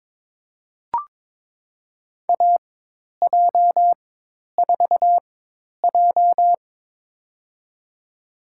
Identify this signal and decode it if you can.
Morse code, heard as a keyed tone, about 22 wpm, spaced as for 10 wpm, 700 Hz. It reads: AJ4J
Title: USA Call Signs - 4 Characters - 4x Character Spacing - 22wpm